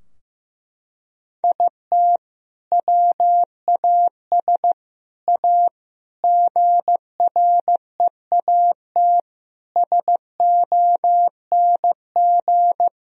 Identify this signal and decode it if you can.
Morse code, heard as a keyed tone, about 16 wpm, 700 Hz. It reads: IT WAS A GREAT SONG
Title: Sentences from Top 1000 Words - Review Part 4 - 15wpm